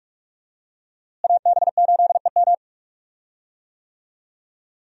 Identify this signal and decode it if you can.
Morse code, heard as a keyed tone, about 46 wpm, 700 Hz. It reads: AB8EM
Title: USA Call Signs - 5 Characters - Repeated 2x - 45wpm